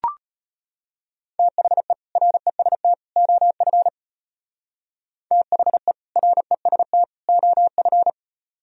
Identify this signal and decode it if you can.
Morse code, heard as a keyed tone, about 32 wpm, 700 Hz. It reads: THE REST OF THE REST OF